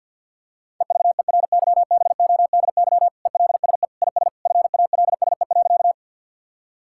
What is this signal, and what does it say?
Morse code, heard as a keyed tone, about 50 wpm, 700 Hz. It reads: EVERYBODY ELSE IS FALSE.
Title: Sentences from Top 1600 Words 50wpm